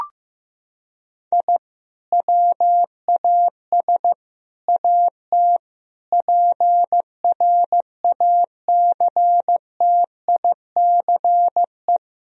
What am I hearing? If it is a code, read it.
Morse code, heard as a keyed tone, about 15 wpm, 700 Hz. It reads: I WAS AT PRACTICE
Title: Sentences from Top 1000 Words - Review Part 4 - 15wpm